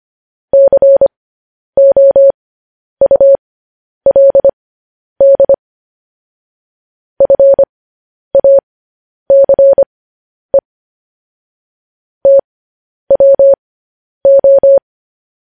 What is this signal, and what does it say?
Morse code, heard as a keyed tone, about 25 wpm, spaced as for 10 wpm, 550 Hz. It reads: COULD FACE TWO
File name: podcast / News headlines-1x-25x10